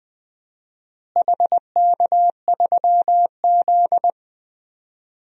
Morse code, heard as a keyed tone, about 20 wpm, 700 Hz. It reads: HK3Z
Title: DXCC-100 Call Signs - 4 Characters - Repeated 2x - 20wpm